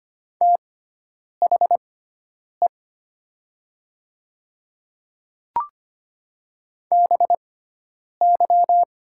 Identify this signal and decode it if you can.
Morse code, heard as a keyed tone, about 25 wpm, spaced as for 9 wpm, 700 Hz. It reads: THE BY
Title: 6x Character Spacing - Top 100 Words - 25wpm